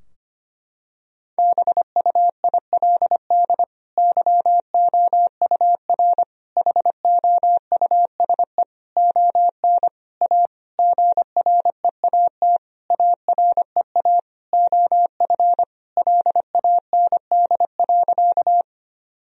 Morse code, heard as a keyed tone, about 25 wpm, 700 Hz. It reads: BUILD YOUR HOUSE ON A GREAT AREA OF LAND.
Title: Sentences from Top 1000 Words - Review Part 2 - 25wpm